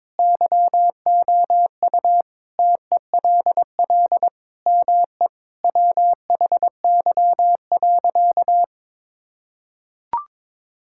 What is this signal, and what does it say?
Morse code, heard as a keyed tone, about 22 wpm, 700 Hz. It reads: YOU TELL ME WHY.
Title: Sentences from Top 1000 Words - Review Part 6 - 22wpm